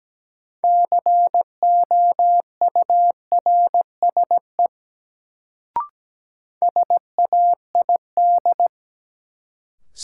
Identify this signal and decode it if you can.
Morse code, heard as a keyed tone, about 17 wpm, 700 Hz. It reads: COURSE SAID
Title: Sets of 2 Words - New Word Drill - 17wpm